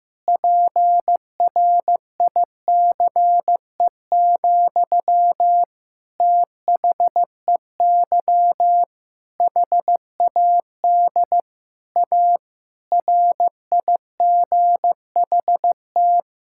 Morse code, heard as a keyed tone, about 15 wpm, 700 Hz. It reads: PRICE, THEY HAD A RIGHT
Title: Sentences from Top 1000 Words - Review Part 5 - 15wpm